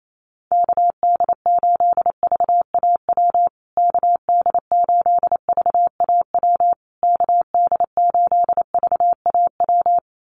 Morse code, heard as a keyed tone, about 28 wpm, 700 Hz. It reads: KD8VAW KD8VAW KD8VAW
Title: USA Call Signs Repeated 3x 28wpm